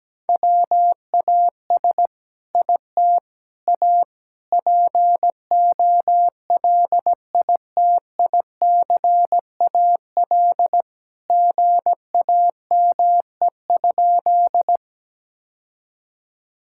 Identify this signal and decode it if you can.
Morse code, heard as a keyed tone, about 17 wpm, 700 Hz. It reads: WAS IT A POLITICAL GAME?
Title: Sentences from Top 1000 Words - Review Part 5 - 17wpm